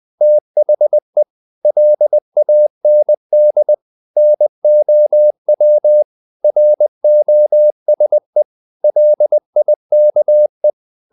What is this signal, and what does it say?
Morse code, heard as a keyed tone, about 20 wpm, 600 Hz. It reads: THE LAND NOW ROSE LIKE
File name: the old man and the sea / the old man and the sea part 1 20wpm